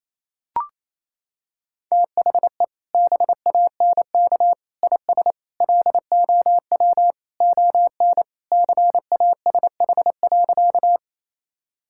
Morse code, heard as a keyed tone, about 28 wpm, 700 Hz. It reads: THE BANK IS LOW ON CASH.
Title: Sentences from Top 1000 Words - Review Part 5 - 28wpm